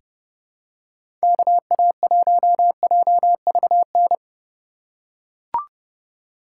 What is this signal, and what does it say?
Morse code, heard as a keyed tone, about 31 wpm, 700 Hz. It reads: KA1JVN
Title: USA Call Signs - 6 Characters - 2x Character Spacing - 30wpm